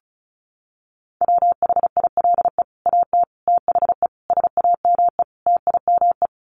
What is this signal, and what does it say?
Morse code, heard as a keyed tone, about 34 wpm, 700 Hz. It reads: WHILE AT THE SAME TIME
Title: Sets of 5 Words 35wpm